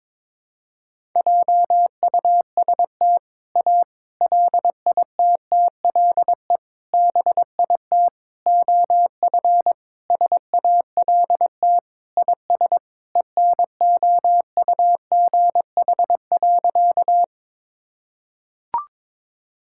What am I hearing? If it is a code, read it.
Morse code, heard as a keyed tone, about 21 wpm, 700 Hz. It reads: JUST A LITTLE BIT OF SALT IS ENOUGH.
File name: practice / 22wpm / Sentences with 4-Word Phrases - 22wpm